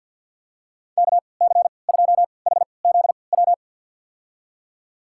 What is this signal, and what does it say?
Morse code, heard as a keyed tone, about 52 wpm, spaced as for 29 wpm, 700 Hz. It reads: KC2HZW
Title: USA Call Signs - 6 Characters - 3x Character Spacing - 50wpm